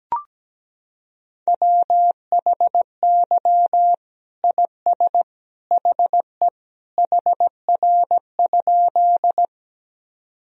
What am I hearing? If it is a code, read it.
Morse code, heard as a keyed tone, about 17 wpm, 700 Hz. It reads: WHY IS HE HR?
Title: Sentences from Top 200 Words and Abbreviations 17wpm